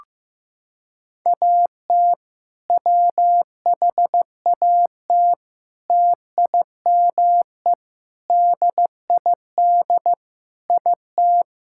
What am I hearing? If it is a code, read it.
Morse code, heard as a keyed tone, about 15 wpm, 700 Hz. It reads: AT WHAT TIME DID IT
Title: Sentences from Top 1000 Words - Review Part 6 - 15wpm